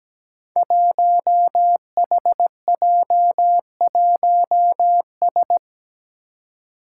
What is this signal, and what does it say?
Morse code, heard as a keyed tone, about 17 wpm, 700 Hz. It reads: 1HJ1S
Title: Five Challenging Letter-Number Combinations 17wpm